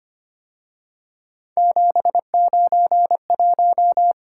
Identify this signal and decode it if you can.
Morse code, heard as a keyed tone, about 29 wpm, 700 Hz. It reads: 791
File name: Speed-Racing - Three Letter-Number Combinations - 25wpm